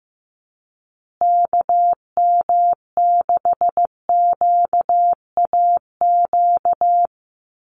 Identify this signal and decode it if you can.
Morse code, heard as a keyed tone, about 14 wpm, 700 Hz. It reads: KM6QAQ
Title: USA Call Signs Repeated 3x 15wpm